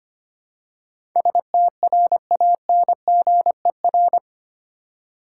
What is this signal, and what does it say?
Morse code, heard as a keyed tone, about 25 wpm, 700 Hz. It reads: STRANGER
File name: 8 Letter Words - First 3 Letters Given - 25wpm